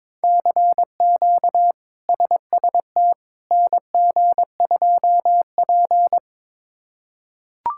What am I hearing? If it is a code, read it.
Morse code, heard as a keyed tone, about 22 wpm, 700 Hz. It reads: CQ SST NG2P
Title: K1USN SST QSO Practice - 22wpm